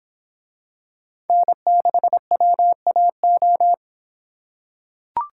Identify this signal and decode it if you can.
Morse code, heard as a keyed tone, about 25 wpm, 700 Hz. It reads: N6WAO